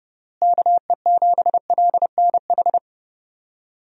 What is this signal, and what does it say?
Morse code, heard as a keyed tone, about 32 wpm, 700 Hz. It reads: KE7LNH
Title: USA Call Signs - 6 Characters - Repeated 3x - 30wpm